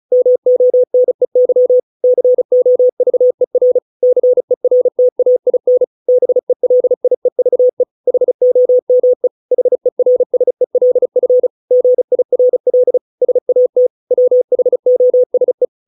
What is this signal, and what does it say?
Morse code, heard as a keyed tone, about 34 wpm, 500 Hz. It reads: MONEY COVER CERTAIN BELIEVE HOME HERSELF GIRL SAT WHOSE